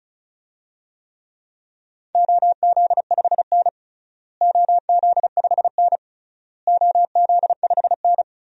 Morse code, heard as a keyed tone, about 34 wpm, 700 Hz. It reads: OZ5N OZ5N OZ5N
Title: DXCC-100 Call Signs - 4 Characters - Repeated 3x - 35wpm